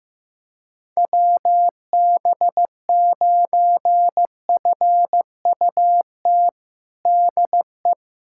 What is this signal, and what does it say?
Morse code, heard as a keyed tone, about 16 wpm, 700 Hz. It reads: WB9FUT DE
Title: USA Callsign DE Callsign - 6 Characters - 15wpm